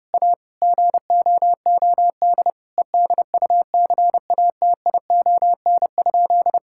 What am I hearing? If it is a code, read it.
Morse code, heard as a keyed tone, about 30 wpm, 700 Hz. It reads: A GOOD EDUCATION?
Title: Sentences from Top 1000 Words - Review Part 6 - 30wpm